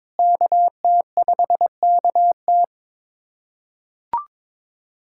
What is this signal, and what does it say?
Morse code, heard as a keyed tone, about 22 wpm, 700 Hz. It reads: KT5KT
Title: USA Callsign DE Callsign - 5 Characters - 22wpm